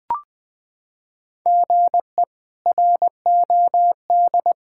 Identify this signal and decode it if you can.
Morse code, heard as a keyed tone, about 20 wpm, 700 Hz. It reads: GE ROD
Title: K1USN SST QSO Practice - 20wpm